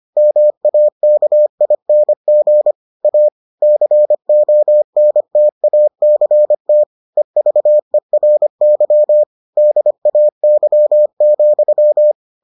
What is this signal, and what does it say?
Morse code, heard as a keyed tone, about 25 wpm, 600 Hz. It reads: MAKING A CONTACT EVERY DAY,